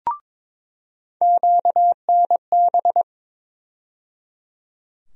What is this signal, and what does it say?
Morse code, heard as a keyed tone, about 22 wpm, 700 Hz. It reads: QNB
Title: USA Call Sign Postfixes - 3 Letters 22wpm